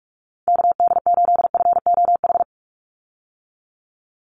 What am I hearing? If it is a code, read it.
Morse code, heard as a keyed tone, about 44 wpm, 690 Hz. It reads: KD8FOH
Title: USA Call Signs 45wpm